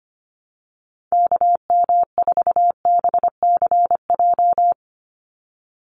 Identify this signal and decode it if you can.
Morse code, heard as a keyed tone, about 25 wpm, 700 Hz. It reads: KM4BCJ
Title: USA Call Signs 25wpm